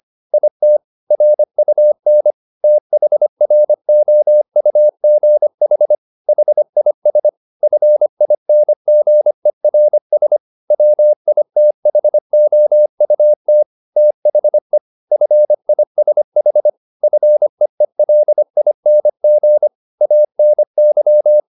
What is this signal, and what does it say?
Morse code, heard as a keyed tone, about 25 wpm, 600 Hz. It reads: IT RUN THROUGH HIS FINGERS WITHOUT THE FISH FEELING ANY
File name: the old man and the sea part 1 25wpm